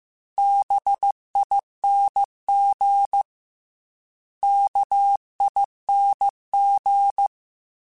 Morse code, heard as a keyed tone, about 15 wpm, 800 Hz. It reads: BING KING
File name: Special INGs-mp3s / Special INGs-mp3s / Sngl ing @13